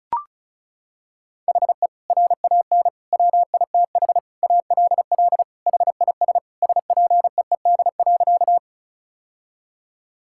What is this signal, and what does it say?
Morse code, heard as a keyed tone, about 36 wpm, 700 Hz. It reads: HE RAN WITH ALL HIS SPEED.
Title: Sentences from Top 1000 Words - Review Part 5 - 35wpm